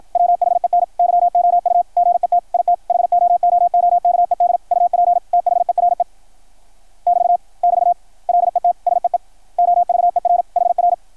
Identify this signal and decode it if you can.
Morse code, heard as a keyed tone, about 53 wpm, 660 Hz. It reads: WHEN YOU GET IT HOOOKED UP THERE <BT> <BT> LET SEE OVER HR
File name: archived / CRSnet